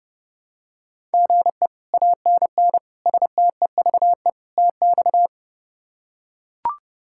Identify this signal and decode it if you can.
Morse code, heard as a keyed tone, about 28 wpm, 700 Hz. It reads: GE ANN STEVE TX
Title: K1USN SST QSO Practice - 30wpm